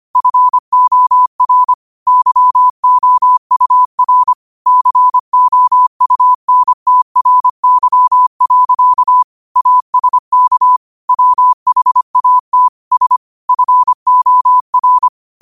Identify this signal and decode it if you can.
Morse code, heard as a keyed tone, about 25 wpm, 1000 Hz. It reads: FOR YOUR COUNTRY. ASK WHAT S FOR
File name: qotdfiles / 25wpm